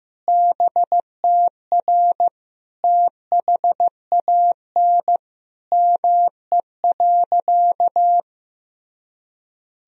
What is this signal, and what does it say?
Morse code, heard as a keyed tone, about 15 wpm, 700 Hz. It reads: BTR THAN ME.
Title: Sentences from Top 400 Words and Abbreviations 15wpm